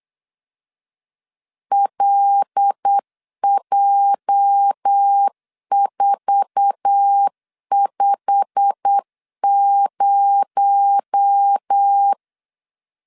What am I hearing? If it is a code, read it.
Morse code, heard as a keyed tone, about 8 wpm, 800 Hz. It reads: LJ450
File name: LEZIONI / 8 YJ45 / 8 YJ45 8wpm 3214